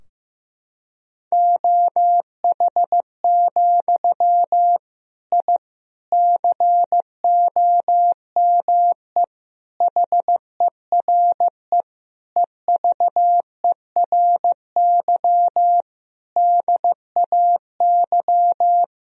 Morse code, heard as a keyed tone, about 16 wpm, 700 Hz. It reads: OH, I COME HERE EVERY DAY
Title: Sentences from Top 1000 Words - Review Part 2 - 15wpm